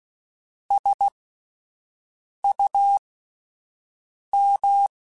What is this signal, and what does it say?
Morse code, heard as a keyed tone, about 16 wpm, spaced as for 5 wpm, 800 Hz. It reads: SUM